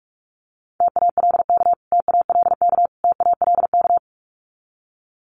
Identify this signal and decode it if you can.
Morse code, heard as a keyed tone, about 44 wpm, 700 Hz. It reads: TALK TALK TALK
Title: Top 500 Words Repeated 45wpm